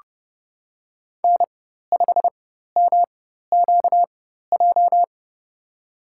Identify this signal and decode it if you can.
Morse code, heard as a keyed tone, about 31 wpm, spaced as for 14 wpm, 700 Hz. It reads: N5MQJ
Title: USA Call Signs - 4x Character Spacing - 30wpm